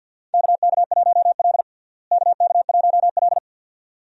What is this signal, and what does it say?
Morse code, heard as a keyed tone, about 51 wpm, 700 Hz. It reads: KK1L KK1L
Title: USA Call Signs - 4 Characters - Repeated 2x - 50wpm